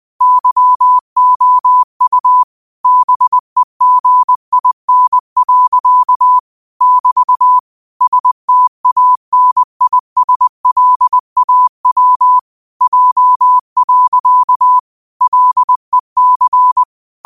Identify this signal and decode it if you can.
Morse code, heard as a keyed tone, about 20 wpm, 1000 Hz. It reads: YOU BEGIN. <BT> STANISLAW J. LEC
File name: qotdfiles / 20wpm